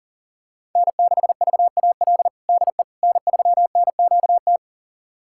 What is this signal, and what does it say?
Morse code, heard as a keyed tone, about 40 wpm, 700 Hz. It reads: N6VAL DE N3NQT
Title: USA Callsign DE Callsign - 5 Characters - 40wpm